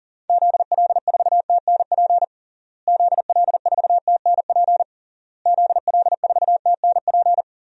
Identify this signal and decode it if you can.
Morse code, heard as a keyed tone, about 40 wpm, 700 Hz. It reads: ZL4TNP ZL4TNP ZL4TNP
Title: DXCC-100 Call Signs - 6 Characters - Repeated 3x - 40wpm